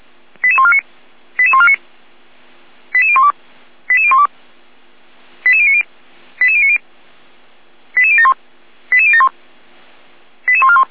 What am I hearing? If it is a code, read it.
Five-tone selective call (ZVEI), heard as quick runs of five tones, tones of about 70 ms. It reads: 88149 88149 88012 88012 88090 88090 88071 88071 88141
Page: /455/SDR/ZVEI/88150/